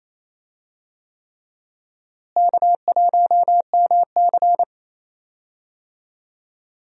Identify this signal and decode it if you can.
Morse code, heard as a keyed tone, about 28 wpm, 700 Hz. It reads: K1MC
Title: USA Call Signs - 4 Characters - Repeated 3x - 28wpm